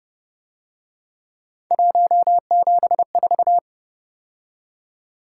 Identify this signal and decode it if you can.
Morse code, heard as a keyed tone, about 29 wpm, 700 Hz. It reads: 174